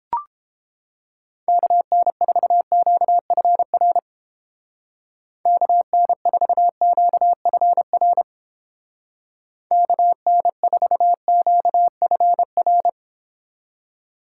Morse code, heard as a keyed tone, about 29 wpm, 700 Hz. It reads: KN4QFR KN4QFR KN4QFR